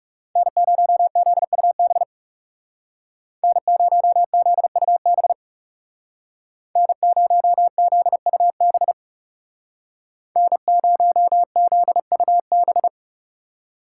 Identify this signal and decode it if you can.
Morse code, heard as a keyed tone, about 35 wpm, 700 Hz. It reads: N0ZUB N0ZUB N0ZUB N0ZUB
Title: Speed Racing - USA 1x3 Call Signs 30wpm